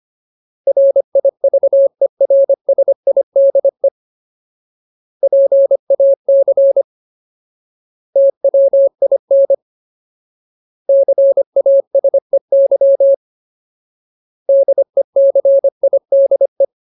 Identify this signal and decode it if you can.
Morse code, heard as a keyed tone, about 25 wpm, 550 Hz. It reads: RIVERSIDE PAC TWIN CASEY DECIDE
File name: Ultimate Word List / 25WPM Ultimate